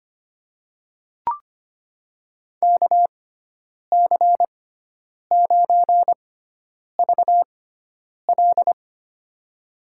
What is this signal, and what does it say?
Morse code, heard as a keyed tone, about 25 wpm, spaced as for 9 wpm, 700 Hz. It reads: KC9VL
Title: USA Call Signs - 6x Character Spacing - 25wpm